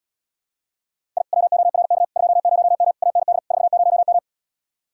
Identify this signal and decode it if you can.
Morse code, heard as a keyed tone, about 77 wpm, 700 Hz. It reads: I WORK 20M AND 40M